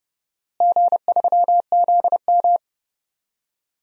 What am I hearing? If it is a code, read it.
Morse code, heard as a keyed tone, about 30 wpm, 700 Hz. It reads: G3ZM
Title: DXCC-100 Call Signs - 4 Characters - Repeated 3x - 30wpm